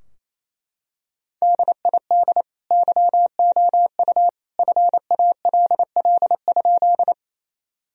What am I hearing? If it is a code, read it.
Morse code, heard as a keyed tone, about 28 wpm, 700 Hz. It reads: DID YOU FALL?
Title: Sentences from Top 1000 Words - Review Part 2 - 28wpm